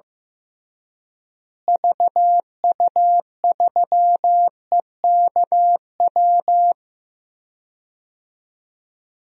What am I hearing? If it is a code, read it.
Morse code, heard as a keyed tone, about 15 wpm, 700 Hz. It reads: VU3EKW
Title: DXCC-100 Call Signs - 6 Characters - Repeated 3x - 15wpm